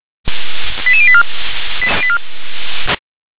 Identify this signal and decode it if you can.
Five-tone selective call (ZVEI), heard as quick runs of five tones, tones of about 70 ms. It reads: 88084
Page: SDR/ZVEI